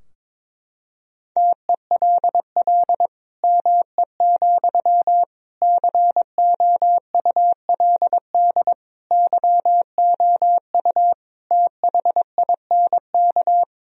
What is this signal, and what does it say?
Morse code, heard as a keyed tone, about 22 wpm, 700 Hz. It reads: TELL ME, COULD YOU THINK
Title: Sentences from Top 1000 Words - Review Part 2 - 22wpm